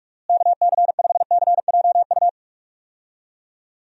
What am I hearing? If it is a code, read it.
Morse code, heard as a keyed tone, about 46 wpm, 700 Hz. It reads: KC5CJU
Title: USA Call Signs - 6 Characters - Repeated 3x - 45wpm